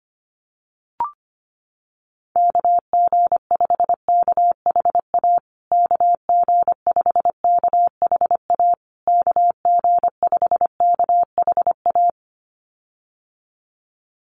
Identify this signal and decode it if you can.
Morse code, heard as a keyed tone, about 25 wpm, 700 Hz. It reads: KG5KHA KG5KHA KG5KHA
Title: USA Call Signs Repeated 3x 25wpm